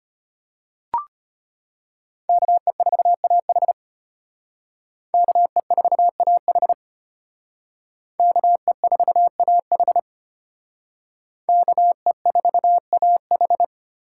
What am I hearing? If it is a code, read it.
Morse code, heard as a keyed tone, about 33 wpm, 700 Hz. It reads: KE4AH KE4AH KE4AH KE4AH